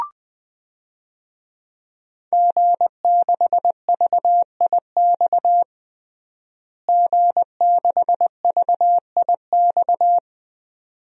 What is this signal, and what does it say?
Morse code, heard as a keyed tone, about 20 wpm, 700 Hz. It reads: G6VIX G6VIX
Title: DXCC-100 Call Signs - 5 Characters - Repeated 2x - 20wpm